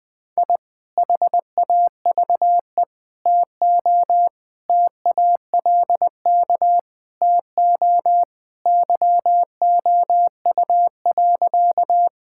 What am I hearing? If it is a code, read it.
Morse code, heard as a keyed tone, about 20 wpm, 700 Hz. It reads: I HAVE TO TALK TO YOU.
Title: Sentences with 3-Word Phrases - 20wpm